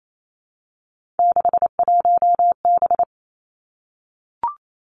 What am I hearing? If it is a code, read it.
Morse code, heard as a keyed tone, about 28 wpm, 700 Hz. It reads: 61B
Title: Three Challenging Letter-Number Combinations 28wpm